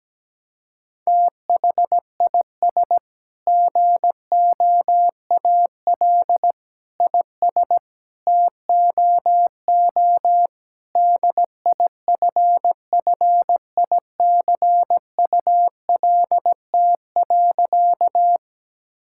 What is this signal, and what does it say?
Morse code, heard as a keyed tone, about 17 wpm, 700 Hz. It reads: THIS GOAL IS TOO DIFFICULT.
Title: Sentences from Top 1000 Words - Review Part 5 - 17wpm